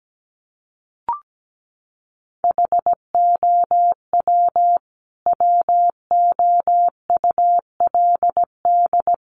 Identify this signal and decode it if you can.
Morse code, heard as a keyed tone, about 17 wpm, 700 Hz. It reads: HOW WOULD
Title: Sentences from Top 100 Words 17wpm